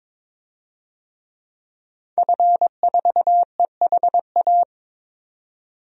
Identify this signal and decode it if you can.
Morse code, heard as a keyed tone, about 22 wpm, 700 Hz. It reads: F4EHA
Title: DXCC-100 Call Signs - 5 Characters - 22wpm